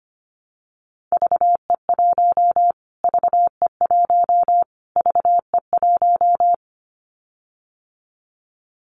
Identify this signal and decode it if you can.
Morse code, heard as a keyed tone, about 25 wpm, 700 Hz. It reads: VE1 VE1 VE1
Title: Callsign Prefixes for Top 25 DXCC - Repeated 3x - 25wpm